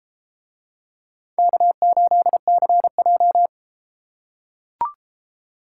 Morse code, heard as a keyed tone, about 31 wpm, 700 Hz. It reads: K8CJ